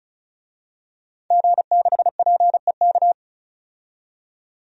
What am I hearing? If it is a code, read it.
Morse code, heard as a keyed tone, about 34 wpm, 700 Hz. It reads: G6PEK
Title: DXCC-100 Call Signs - 5 Characters - 35wpm